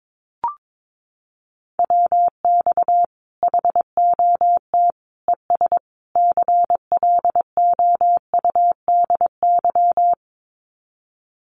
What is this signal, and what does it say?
Morse code, heard as a keyed tone, about 22 wpm, 700 Hz. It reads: WX HOT ES CLOUDY
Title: Common QSO Phrases 22wpm